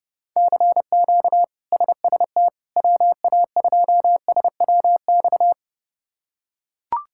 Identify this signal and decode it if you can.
Morse code, heard as a keyed tone, about 32 wpm, 700 Hz. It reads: CQ SST WA2SWX
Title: K1USN SST QSO Practice - 30wpm